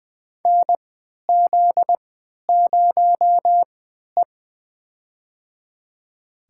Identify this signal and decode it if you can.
Morse code, heard as a keyed tone, about 20 wpm, spaced as for 11 wpm, 700 Hz. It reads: NZ0E
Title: USA Call Signs - 4 Characters - 3x Character Spacing - 20wpm